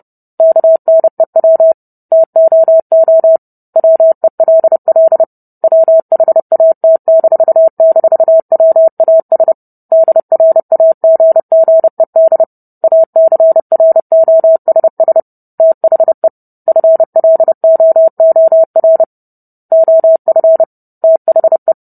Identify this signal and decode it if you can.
Morse code, heard as a keyed tone, about 30 wpm, 650 Hz. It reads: KNEW TOO WELL WHAT--WAS DRAGGED ACROSS THE FLOOR OF THE